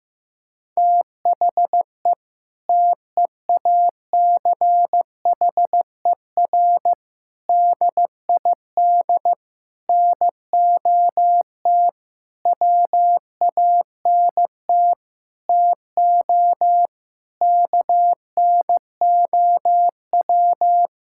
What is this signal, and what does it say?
Morse code, heard as a keyed tone, about 15 wpm, 700 Hz. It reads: THE TEACHER DID NOT WANT TO KNOW
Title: Sentences from Top 1000 Words - Review Part 6 - 15wpm